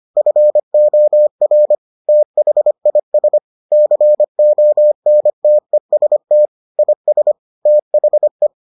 Morse code, heard as a keyed tone, about 25 wpm, 600 Hz. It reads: FOR THIS CONTEST IS THE